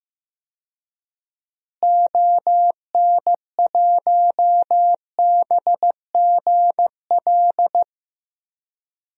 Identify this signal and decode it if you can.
Morse code, heard as a keyed tone, about 15 wpm, 700 Hz. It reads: ON1BGL
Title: DXCC-100 Call Signs - 6 Characters - Repeated 2x - 15wpm